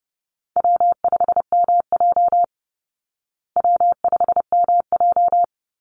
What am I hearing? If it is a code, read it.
Morse code, heard as a keyed tone, about 30 wpm, 700 Hz. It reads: W5MJ W5MJ
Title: Easy Call Signs Repeated 2x - 30wpm